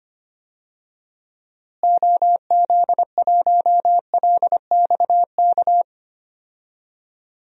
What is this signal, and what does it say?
Morse code, heard as a keyed tone, about 25 wpm, 700 Hz. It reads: OZ1LXK
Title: DXCC-100 Call Signs - 6 Characters - Repeated 2x - 25wpm